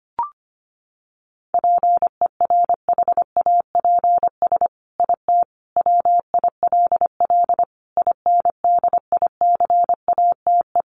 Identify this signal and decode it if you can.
Morse code, heard as a keyed tone, about 25 wpm, 700 Hz. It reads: PERHAPS IT WILL INDICATE
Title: Sentences from Top 1300 Words 25wpm